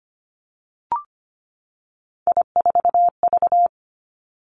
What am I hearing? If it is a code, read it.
Morse code, heard as a keyed tone, about 25 wpm, 700 Hz. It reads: I4V
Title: Three Challenging Letter-Number Combinations 25wpm